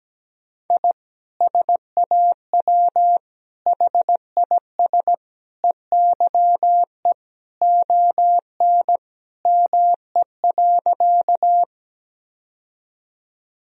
Morse code, heard as a keyed tone, about 17 wpm, 700 Hz. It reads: I SAW HIS EYE ON ME.
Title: Sentences from Top 1000 Words - Review Part 1 - 17wpm